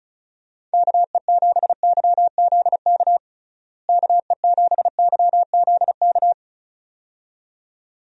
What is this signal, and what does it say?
Morse code, heard as a keyed tone, about 34 wpm, 700 Hz. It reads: KE7YZK KE7YZK
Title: USA Call Signs - 6 Characters - Repeated 2x - 35wpm